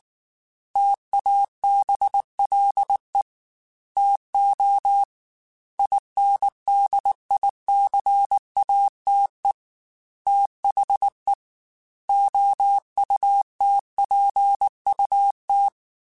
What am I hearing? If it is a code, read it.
Morse code, heard as a keyed tone, about 19 wpm, 800 Hz. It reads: TABLE TO INDICATE THE OUTPUT
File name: Technical Subjects / Technical Subjects-mp3s / Gates Intro @18